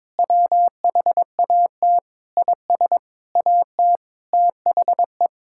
Morse code, heard as a keyed tone, about 22 wpm, 700 Hz. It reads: WHAT IS AT THE
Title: Sentences from Top 1000 Words - Review Part 6 - 22wpm